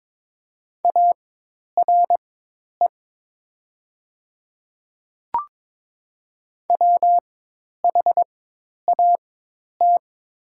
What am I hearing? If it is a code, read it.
Morse code, heard as a keyed tone, about 22 wpm, spaced as for 10 wpm, 700 Hz. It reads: ARE WHAT
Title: 4x Character Spacing - Top 100 Words - 22wpm